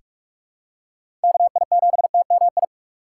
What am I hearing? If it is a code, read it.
Morse code, heard as a keyed tone, about 44 wpm, 700 Hz. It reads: KI7TMI
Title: USA Call Signs - 6 Characters - 2x Character Spacing - 45wpm